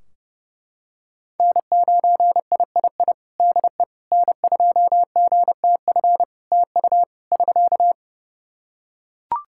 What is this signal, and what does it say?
Morse code, heard as a keyed tone, about 30 wpm, 700 Hz. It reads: N9III DE N2GTF TU <SK>
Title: Practice QSO - Exchange 4 - Ending - 30wpm